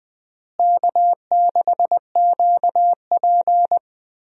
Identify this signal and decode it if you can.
Morse code, heard as a keyed tone, about 20 wpm, 700 Hz. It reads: K6QP
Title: USA Call Signs - 4 Characters - Repeated 2x - 20wpm